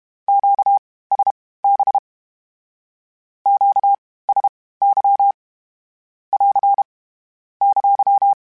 Morse code, heard as a keyed tone, about 32 wpm, spaced as for 18 wpm, 800 Hz. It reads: QSB QSY <AR> !
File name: (spaced) / Q-Calls Spaced-mp3s / Q Calls sp Once @30